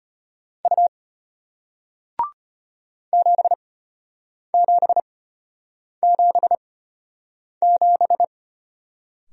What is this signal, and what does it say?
Morse code, heard as a keyed tone, about 34 wpm, spaced as for 8 wpm, 700 Hz. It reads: U 7777